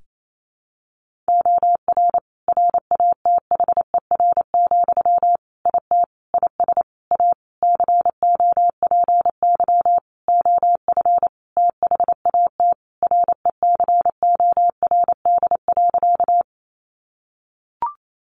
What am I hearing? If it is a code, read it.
Morse code, heard as a keyed tone, about 28 wpm, 700 Hz. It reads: OR RATHER, IT IS A COPY OF THAT RECORD.
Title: Sentences from Top 1400 Words 28wpm